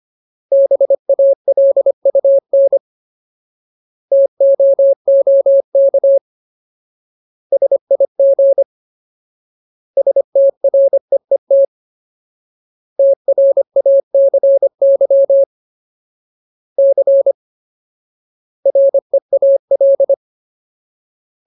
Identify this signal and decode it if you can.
Morse code, heard as a keyed tone, about 25 wpm, 550 Hz. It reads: BALUN TOOK SIG STREET TRACY C REAL
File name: Ultimate Word List / 25WPM Ultimate